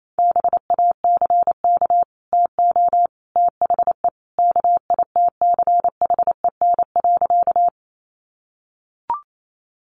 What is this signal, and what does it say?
Morse code, heard as a keyed tone, about 28 wpm, 700 Hz. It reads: BACK TO THE KITCHEN.
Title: Sentences from Top 1400 Words 28wpm